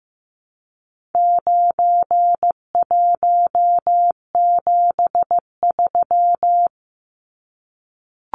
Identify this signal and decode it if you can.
Morse code, heard as a keyed tone, about 15 wpm, 700 Hz. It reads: 9173
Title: Numbers - 4-Digit 15wpm